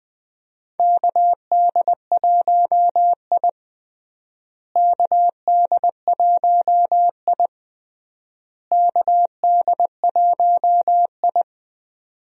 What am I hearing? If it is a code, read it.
Morse code, heard as a keyed tone, about 20 wpm, 700 Hz. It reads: KD1I KD1I KD1I
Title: USA Call Signs - 4 Characters - Repeated 3x - 20wpm